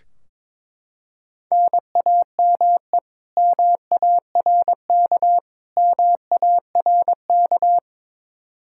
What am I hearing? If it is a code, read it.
Morse code, heard as a keyed tone, about 21 wpm, 700 Hz. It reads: NAME MARK MARK
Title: Practice QSO - Exchange 1 - Beginning - 22wpm